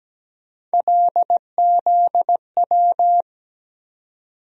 Morse code, heard as a keyed tone, about 17 wpm, 700 Hz. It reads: LZW
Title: USA Call Sign Postfixes - 3 Letters 17wpm